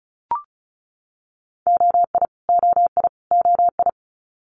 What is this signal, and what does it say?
Morse code, heard as a keyed tone, about 36 wpm, 700 Hz. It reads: OI OI OI
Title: Multi-letter Phonemes Repeated 3x 35wpm